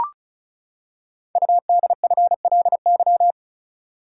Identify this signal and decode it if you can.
Morse code, heard as a keyed tone, about 36 wpm, 700 Hz. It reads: UDFLY
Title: Code Groups - 5 Letters - 35wpm